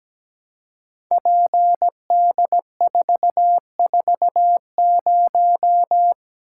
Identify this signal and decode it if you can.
Morse code, heard as a keyed tone, about 17 wpm, 700 Hz. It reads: PD440
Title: Code Groups - 5 Letter:Numbers - 17wpm